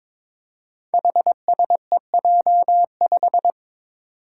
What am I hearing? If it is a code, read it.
Morse code, heard as a keyed tone, about 22 wpm, 700 Hz. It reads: HSEJ5
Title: Five Challenging Letter-Number Combinations 22wpm